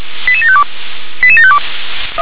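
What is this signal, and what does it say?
Five-tone selective call (ZVEI), heard as quick runs of five tones, tones of about 70 ms. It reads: 88752 88752
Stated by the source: ZVEI five-tone call received by SDR.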